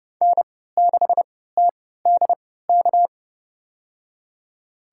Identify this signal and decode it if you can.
Morse code, heard as a keyed tone, about 31 wpm, spaced as for 17 wpm, 700 Hz. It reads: N6TDK
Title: USA Call Signs - 5 Characters - 3x Character Spacing - 30wpm